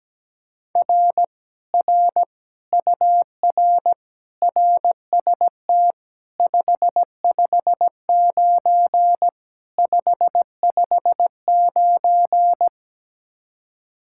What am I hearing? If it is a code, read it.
Morse code, heard as a keyed tone, about 17 wpm, 700 Hz. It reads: R R UR RST 559 559
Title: Speed-Racing Common QSO Phrases - 17wpm